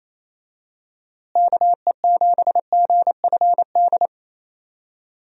Morse code, heard as a keyed tone, about 28 wpm, 700 Hz. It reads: KE7GFD
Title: USA Call Signs - 6 Characters - Repeated 3x - 28wpm